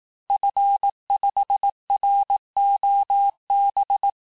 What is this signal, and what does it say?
Morse code, heard as a keyed tone, about 18 wpm, 800 Hz. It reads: F5ROB
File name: cours-du-lundi files / cours-1588-FIN-de-QSO-vitesse-17MPM